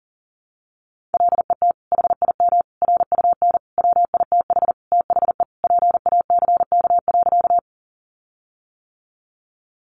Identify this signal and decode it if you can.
Morse code, heard as a keyed tone, about 40 wpm, 700 Hz. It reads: LET HIM RUN WITH THE PACK.
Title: Sentences from Top 1300 Words 40wpm